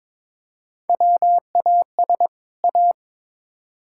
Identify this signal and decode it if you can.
Morse code, heard as a keyed tone, about 21 wpm, 700 Hz. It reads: WAS A
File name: practice / 22wpm / Build A Phrase - 5-Word Phrases - 22wpm